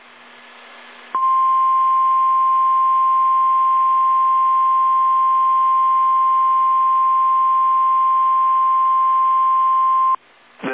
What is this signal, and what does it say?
Emergency Alert System (SAME) tones (the 1050 Hz attention tone). It is heard from 1.1 to 10.2 s.
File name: eas / KSDZKDJL / KSDZKDJL